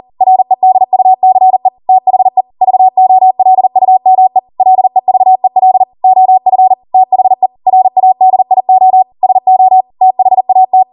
Morse code, heard as a keyed tone, about 40 wpm, 750 Hz. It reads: REDUCE THE VOLUME LEVEL OF THE RADIO SO THAT
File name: Morse / 40WPM